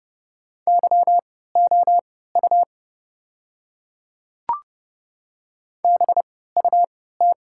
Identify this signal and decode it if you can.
Morse code, heard as a keyed tone, about 30 wpm, spaced as for 17 wpm, 700 Hz. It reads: YOU BUT
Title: 3x Character Spacing - Top 100 Words - 30wpm